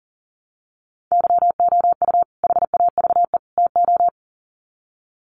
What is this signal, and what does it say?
Morse code, heard as a keyed tone, about 40 wpm, 700 Hz. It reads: YOU HAVE TO
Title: Sets of 3 Words 40wpm